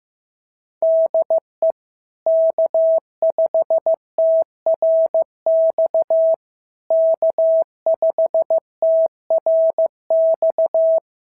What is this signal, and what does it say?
Morse code, heard as a keyed tone, about 15 wpm, 690 Hz. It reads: DE K5TRX K5TRX
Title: SOTA QSO Practice 15wpm